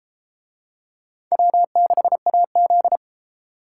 Morse code, heard as a keyed tone, about 33 wpm, 700 Hz. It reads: W6AZ